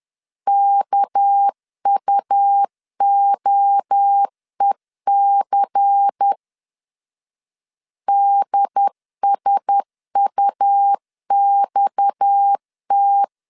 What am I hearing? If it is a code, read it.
Morse code, heard as a keyed tone, about 10 wpm, 800 Hz. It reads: KUOEC DSUXT
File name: LEZIONI / 6 PKX0 / 6 PKX 10wpm 2426